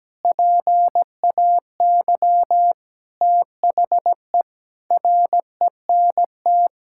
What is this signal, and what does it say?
Morse code, heard as a keyed tone, about 17 wpm, 700 Hz. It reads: PAY THE RENT
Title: Sentences from Top 1000 Words - Review Part 5 - 17wpm